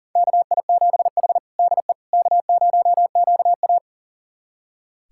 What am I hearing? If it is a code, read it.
Morse code, heard as a keyed tone, about 37 wpm, 700 Hz. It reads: KI7H DE K0QA
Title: USA Callsign DE Callsign - 4 Characters - 40wpm